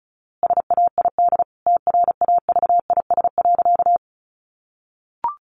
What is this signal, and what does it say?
Morse code, heard as a keyed tone, about 34 wpm, 700 Hz. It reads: SAID TRAVIS.
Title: A Sound Of Thunder 35wpm